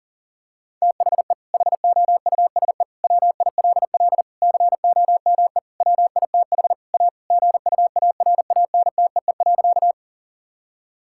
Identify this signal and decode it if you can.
Morse code, heard as a keyed tone, about 40 wpm, 700 Hz. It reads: THE HOUSE WILL COME WITH A GUARANTEE.
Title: Sentences from Top 1600 Words 40wpm